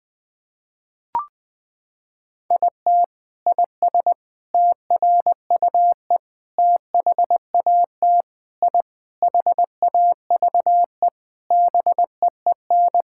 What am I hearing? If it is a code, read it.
Morse code, heard as a keyed tone, about 20 wpm, 700 Hz. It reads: IT IS TRUE THAT I HAVE BEEN
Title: Sentences from Top 1000 Words - Review Part 6 - 20wpm